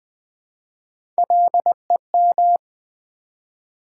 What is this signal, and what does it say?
Morse code, heard as a keyed tone, about 20 wpm, 700 Hz. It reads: LEM
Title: Build A Word - 1 Letter At A Time - 5-Letter Words - 20wpm